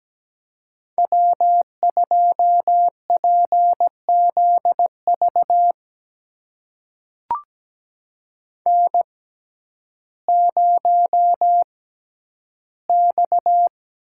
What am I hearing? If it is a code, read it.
Morse code, heard as a keyed tone, about 17 wpm, spaced as for 10 wpm, 700 Hz. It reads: W2PZV N0X
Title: USA Call Signs - 5 Characters - 6x Character Spacing - 17wpm